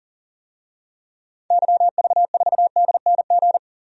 Morse code, heard as a keyed tone, about 40 wpm, 700 Hz. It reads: YV4DNG
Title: DXCC-100 Call Signs - 6 Characters - 40wpm